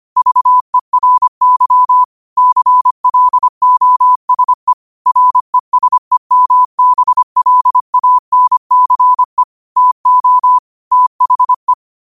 Morse code, heard as a keyed tone, about 25 wpm, 1000 Hz. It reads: VERY CLOSE RESEMBLANCE TO THE